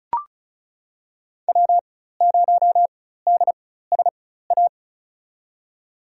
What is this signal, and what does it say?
Morse code, heard as a keyed tone, about 36 wpm, spaced as for 16 wpm, 700 Hz. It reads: W0DSA
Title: USA Call Signs - 5 Characters - 4x Character Spacing - 35wpm